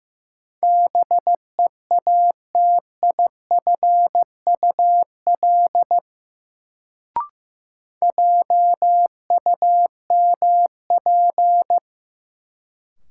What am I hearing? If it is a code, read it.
Morse code, heard as a keyed tone, about 15 wpm, 700 Hz. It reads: BEATIFUL JUMP
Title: Sentences from Top 1300 Words - New Word Drill Alt - 15wpm